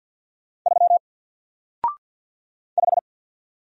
Morse code, heard as a keyed tone, about 50 wpm, 700 Hz. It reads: 3 5
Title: Single Letter-Number Alt 50wpm